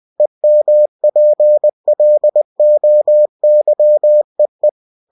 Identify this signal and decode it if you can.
Morse code, heard as a keyed tone, about 20 wpm, 600 Hz. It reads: EMPLOYEE
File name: cw-books / winnie-the-pooh / winnie-20wpm-page-0120